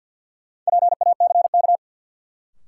Morse code, heard as a keyed tone, about 52 wpm, 700 Hz. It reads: PACK
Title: Sentences from Top 1300 Words - New Word Drill - 50wpm